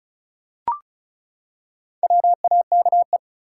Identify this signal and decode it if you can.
Morse code, heard as a keyed tone, about 36 wpm, 700 Hz. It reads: WAKE
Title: Four Letter Words - 35wpm